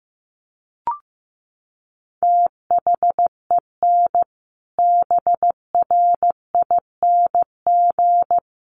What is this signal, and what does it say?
Morse code, heard as a keyed tone, about 15 wpm, 700 Hz. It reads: THEN BRING
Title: Sentences from Top 500 Words 15wpm